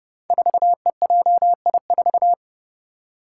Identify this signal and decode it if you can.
Morse code, heard as a keyed tone, about 30 wpm, 700 Hz. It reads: 4EJI4
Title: Five Challenging Letter-Number Combinations 30wpm